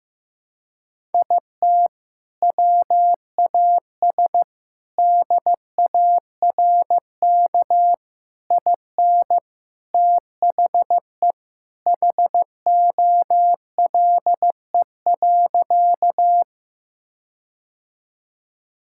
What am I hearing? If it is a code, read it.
Morse code, heard as a keyed tone, about 15 wpm, 700 Hz. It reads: IT WAS DARK IN THE HOLE.
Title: Sentences from Top 1600 Words 15wpm